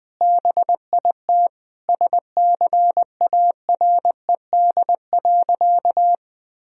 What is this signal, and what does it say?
Morse code, heard as a keyed tone, about 20 wpm, 700 Hz. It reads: BIT SCARED.
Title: Sentences from Top 1000 Words - Review Part 4 - 20wpm